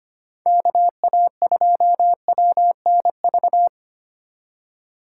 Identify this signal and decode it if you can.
Morse code, heard as a keyed tone, about 25 wpm, 700 Hz. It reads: KA2WNV
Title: USA Call Signs - 3x Character Spacing - 25wpm